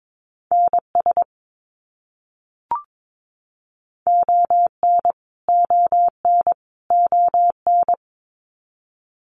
Morse code, heard as a keyed tone, about 22 wpm, 700 Hz. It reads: NS ON ON ON
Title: Canadian Provinces and Territories - Repeated 3x - 22wpm